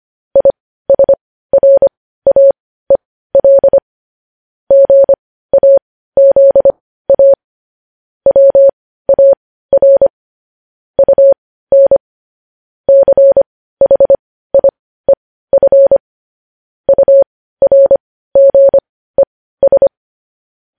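Morse code, heard as a keyed tone, about 25 wpm, spaced as for 15 wpm, 550 Hz. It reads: ISRAEL GAZA WAR UN CHIEF URGES
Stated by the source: Morse code transcription of news headlines.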